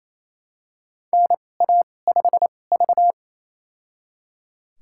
Morse code, heard as a keyed tone, about 28 wpm, spaced as for 20 wpm, 700 Hz. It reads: NA5V
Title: USA Call Signs - 4 Characters - 2x Character Spacing - 28wpm